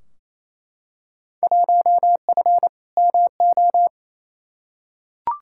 Morse code, heard as a keyed tone, about 28 wpm, 700 Hz. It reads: 1F MO
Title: Field Day Exchanges Encore - 28wpm